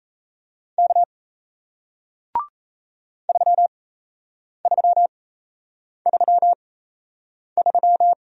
Morse code, heard as a keyed tone, about 38 wpm, spaced as for 8 wpm, 700 Hz. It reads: K 3333